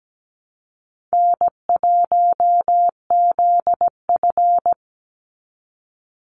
Morse code, heard as a keyed tone, about 17 wpm, 700 Hz. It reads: N1ZF
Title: NA SOTA Chaser Callsigns 17wpm